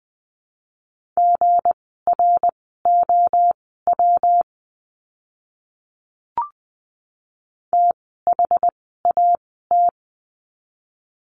Morse code, heard as a keyed tone, about 20 wpm, spaced as for 15 wpm, 700 Hz. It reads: GROW THAT
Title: 160 Lesson 26f - Letter g - Words 20wpm